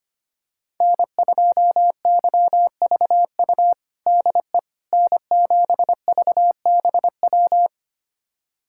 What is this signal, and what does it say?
Morse code, heard as a keyed tone, about 25 wpm, 700 Hz. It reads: N2YVU DE N7VBW
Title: USA Callsign DE Callsign 25wpm